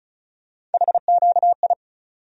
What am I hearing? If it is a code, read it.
Morse code, heard as a keyed tone, about 36 wpm, 700 Hz. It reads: HQI
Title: USA Call Sign Postfixes - 3 Letters 35wpm